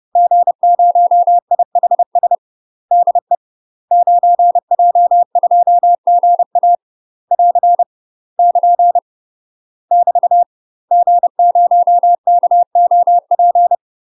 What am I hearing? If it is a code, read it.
Morse code, heard as a keyed tone, about 30 wpm, 700 Hz. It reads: G0IHS DE 9J2GA <AR> <KN> <BT> G0KOP
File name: UK MASTER QSOs TRAINING / 30WPM / UK MASTER QSOs 0024